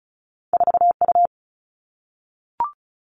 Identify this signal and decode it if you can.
Morse code, heard as a keyed tone, about 36 wpm, 700 Hz. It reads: 4U
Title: All Cut Numbers - 2-Digit 35wpm